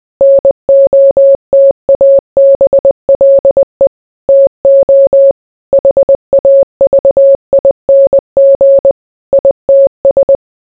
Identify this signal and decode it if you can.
Morse code, heard as a keyed tone, about 20 wpm, 550 Hz. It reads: NOTABLE TO HAVING ITS